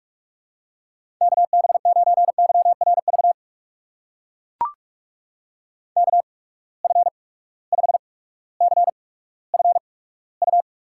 Morse code, heard as a keyed tone, about 42 wpm, spaced as for 19 wpm, 700 Hz. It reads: KB9YRV KF5CFU
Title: USA Call Signs - 6 Characters - 8x Character Spacing - 45wpm